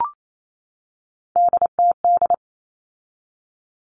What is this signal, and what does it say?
Morse code, heard as a keyed tone, about 28 wpm, 700 Hz. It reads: DTD
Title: All Cut Numbers - 3-Digit 28wpm